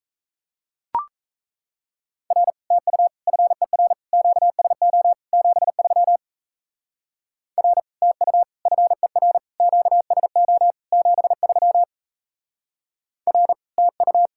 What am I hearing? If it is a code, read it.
Morse code, heard as a keyed tone, about 34 wpm, 700 Hz. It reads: R TU FER QSO 73 R TU FER QSO 73 R TU
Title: Speed-Racing Common QSO Phrases Encore - 28wpm